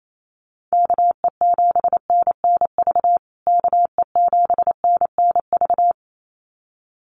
Morse code, heard as a keyed tone, about 28 wpm, 700 Hz. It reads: KE7NNV KE7NNV
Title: USA Call Signs Repeated 2x 28wpm